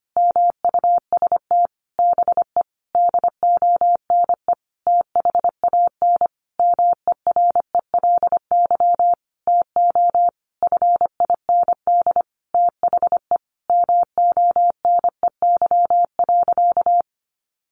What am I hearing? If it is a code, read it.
Morse code, heard as a keyed tone, about 25 wpm, 700 Hz. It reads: MUST BE DONE THAN MERELY TO FIND THE MONEY.
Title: Sentences from Top 1400 Words 25wpm